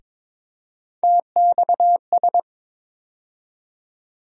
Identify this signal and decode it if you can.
Morse code, heard as a keyed tone, about 22 wpm, 700 Hz. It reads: TXS
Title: USA Call Sign Postfixes - 3 Letters 22wpm